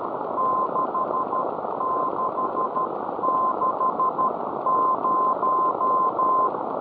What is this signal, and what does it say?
Morse code, heard as a keyed tone, about 12 wpm, 1040 Hz, with noise. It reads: 6660